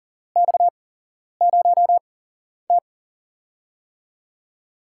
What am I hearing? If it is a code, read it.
Morse code, heard as a keyed tone, about 38 wpm, spaced as for 11 wpm, 700 Hz. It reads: X0T
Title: USA Call Signs - 8x Character Spacing - 40wpm